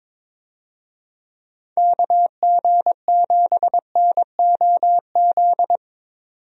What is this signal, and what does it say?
Morse code, heard as a keyed tone, about 21 wpm, 700 Hz. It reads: KG7NOZ
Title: USA Call Signs - 6 Characters - 22wpm